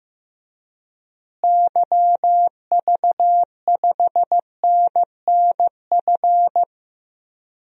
Morse code, heard as a keyed tone, about 15 wpm, 700 Hz. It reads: YV5NNF
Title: DXCC-100 Call Signs - 6 Characters - Repeated 3x - 15wpm